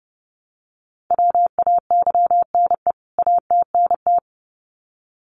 Morse code, heard as a keyed tone, about 30 wpm, 700 Hz. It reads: WAYNE ATNT
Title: CWT Exchange 30wpm